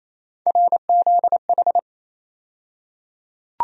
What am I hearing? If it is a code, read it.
Morse code, heard as a keyed tone, about 28 wpm, 700 Hz. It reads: RZH
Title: USA Call Sign Postfixes - 3 Letters 28wpm